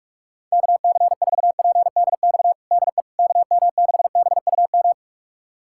Morse code, heard as a keyed tone, about 41 wpm, 700 Hz. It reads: KC4PDX DE KM6BUM
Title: USA Callsign DE Callsign 45wpm